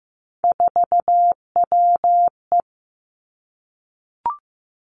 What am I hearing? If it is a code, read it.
Morse code, heard as a keyed tone, about 15 wpm, 700 Hz. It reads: 4WE
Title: Three Challenging Letter-Number Combinations 15wpm